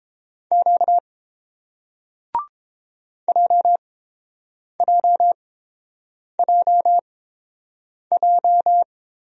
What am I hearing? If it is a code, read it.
Morse code, heard as a keyed tone, about 31 wpm, spaced as for 7 wpm, 690 Hz. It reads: Q JJJJ